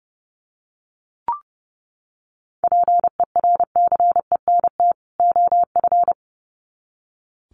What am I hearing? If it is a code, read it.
Morse code, heard as a keyed tone, about 30 wpm, 700 Hz. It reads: PERCENT OF
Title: Sets of 2 Words 30wpm